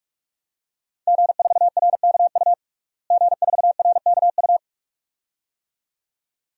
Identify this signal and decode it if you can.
Morse code, heard as a keyed tone, about 42 wpm, 700 Hz. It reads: G4RKU G4RKU
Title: DXCC-100 Call Signs - 5 Characters - Repeated 2x - 45wpm